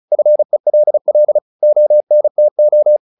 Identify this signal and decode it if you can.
Morse code, heard as a keyed tone, about 34 wpm, 600 Hz. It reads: FELL ONTO
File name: the old man and the sea 2 35wpm